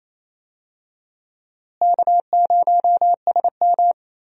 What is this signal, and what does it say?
Morse code, heard as a keyed tone, about 28 wpm, 700 Hz. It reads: K0SM
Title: USA Call Signs - 4 Characters - Repeated 2x - 28wpm